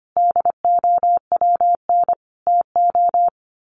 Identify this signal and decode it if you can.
Morse code, heard as a keyed tone, about 25 wpm, 700 Hz. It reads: DOWN TO
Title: A Sound Of Thunder 25wpm